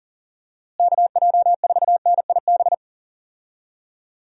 Morse code, heard as a keyed tone, about 40 wpm, 700 Hz. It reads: KJ4NIB
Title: USA Call Signs - 6 Characters - Repeated 3x - 40wpm